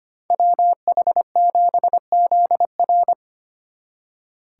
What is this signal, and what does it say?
Morse code, heard as a keyed tone, about 30 wpm, 700 Hz. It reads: WH7ZR
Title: Speed Racing - USA 2x2 Call Signs 25wpm